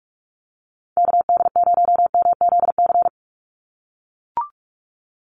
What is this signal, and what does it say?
Morse code, heard as a keyed tone, about 44 wpm, 700 Hz. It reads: KD0MZC
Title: USA Call Signs 45wpm